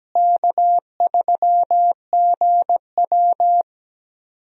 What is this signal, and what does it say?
Morse code, heard as a keyed tone, about 17 wpm, 700 Hz. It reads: K3GW
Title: USA Call Signs - 4 Characters - Repeated 3x - 17wpm